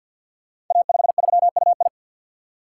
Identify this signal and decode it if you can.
Morse code, heard as a keyed tone, about 46 wpm, 700 Hz. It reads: A53UI
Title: Code Groups - 5 Letter:Numbers - 50wpm